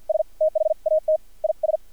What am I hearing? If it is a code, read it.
Morse code, heard as a keyed tone, about 50 wpm, spaced as for 43 wpm, 630 Hz. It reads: S THAT IS
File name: archived / CRSnet